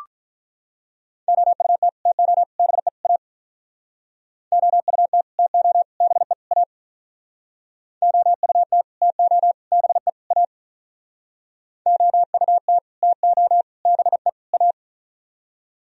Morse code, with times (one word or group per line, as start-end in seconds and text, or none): 1.3-1.9 OUT
2.1-2.4 TO
2.6-2.9 BE
3.0-3.2 A
4.5-5.2 OUT
5.4-5.8 TO
6.0-6.3 BE
6.5-6.6 A
8.0-8.8 OUT
9.0-9.5 TO
9.7-10.1 BE
10.3-10.4 A
11.9-12.8 OUT
13.0-13.6 TO
13.9-14.3 BE
14.5-14.7 A